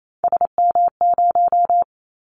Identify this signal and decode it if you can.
Morse code, heard as a keyed tone, about 28 wpm, 700 Hz. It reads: SM0
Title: Callsign Prefixes for Top 25 DXCC - 28wpm